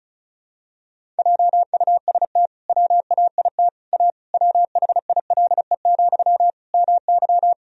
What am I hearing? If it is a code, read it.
Morse code, heard as a keyed tone, about 36 wpm, 700 Hz. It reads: JUST WAIT A WHILE, MY
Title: Sentences from Top 1000 Words - Review Part 1 - 35wpm